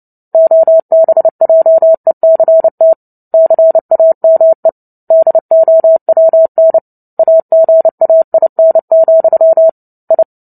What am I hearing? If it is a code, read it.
Morse code, heard as a keyed tone, about 30 wpm, 650 Hz. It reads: OBJECT CAME DOWN AGAIN, I
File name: Book Two / 03-Chapter 3 - 29 WPM